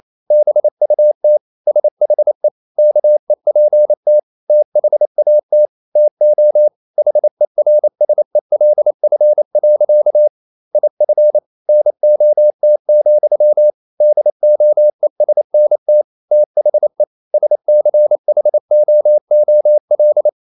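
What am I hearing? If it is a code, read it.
Morse code, heard as a keyed tone, about 28 wpm, 600 Hz. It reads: BUT SHE KEPT THAT TO HERSELF. IF NOT, DOESNT THE SCHOOL
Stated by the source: Morse code practice at 28 wpm, text of Ken Follett's novel Never in English, from CW med Gnister.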